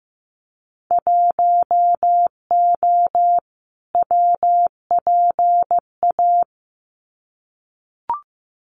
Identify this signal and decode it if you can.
Morse code, heard as a keyed tone, about 15 wpm, 700 Hz. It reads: 1O WPA
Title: Winter Field Day Exchanges 15wpm